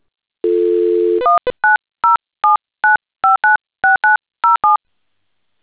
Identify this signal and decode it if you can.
DTMF, heard as telephone touch tones, tones of about 150 ms. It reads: 19079596907